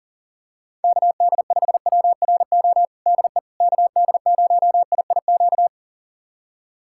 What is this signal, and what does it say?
Morse code, heard as a keyed tone, about 37 wpm, 700 Hz. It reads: KD5WRO DE KD0IIQ
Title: USA Callsign DE Callsign - 6 Characters - 40wpm